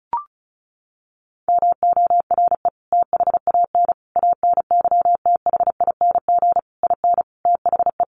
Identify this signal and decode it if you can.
Morse code, heard as a keyed tone, about 34 wpm, 700 Hz. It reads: MORE THAN ANYTHING IN THE
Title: Sentences from Top 1000 Words 35wpm